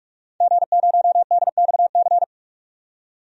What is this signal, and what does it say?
Morse code, heard as a keyed tone, about 41 wpm, 700 Hz. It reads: G0DXC
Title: DXCC-100 Call Signs - 5 Characters - 45wpm